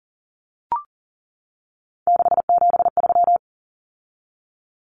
Morse code, heard as a keyed tone, about 40 wpm, 700 Hz. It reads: 673
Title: Common Cut Numbers - 3-Digit 40wpm